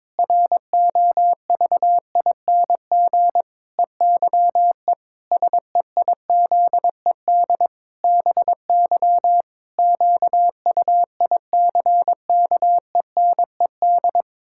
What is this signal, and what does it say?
Morse code, heard as a keyed tone, about 22 wpm, 700 Hz. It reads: ROVING EYE SEIZED BY QUICKENED
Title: Pangram Challenge v2 - 22wpm